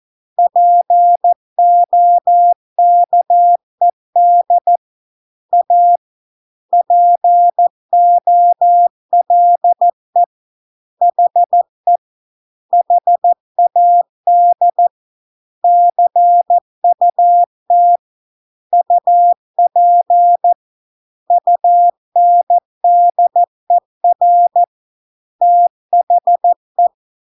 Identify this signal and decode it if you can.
Morse code, heard as a keyed tone, about 14 wpm, 700 Hz. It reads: POKED A POLE HE HAD CUT UP UNDER THE